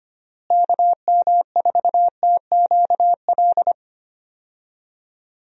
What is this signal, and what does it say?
Morse code, heard as a keyed tone, about 25 wpm, 700 Hz. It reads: KM4TQL
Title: USA Call Signs - 6 Characters - Repeated 3x - 25wpm